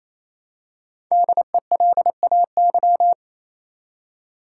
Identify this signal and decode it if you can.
Morse code, heard as a keyed tone, about 28 wpm, 700 Hz. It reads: DELAY
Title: Five Letter Words - 28wpm